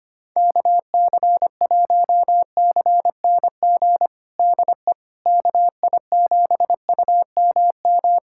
Morse code, heard as a keyed tone, about 25 wpm, 700 Hz. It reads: KC1CNG DE KI7UMM
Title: USA Callsign DE Callsign 25wpm